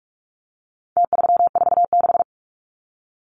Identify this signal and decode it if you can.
Morse code, heard as a keyed tone, about 44 wpm, 700 Hz. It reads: T346
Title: Numbers - Contest Style - 4-Digit 45wpm